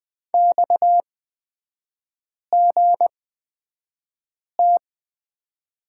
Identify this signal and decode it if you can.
Morse code, heard as a keyed tone, about 20 wpm, spaced as for 5 wpm, 690 Hz. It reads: XGT
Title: 229 Lesson 37e - Letter x - Three Characters 20wpm